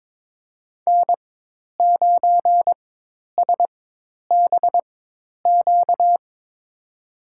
Morse code, heard as a keyed tone, about 21 wpm, spaced as for 10 wpm, 700 Hz. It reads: N9SBQ
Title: USA Call Signs - 4x Character Spacing - 22wpm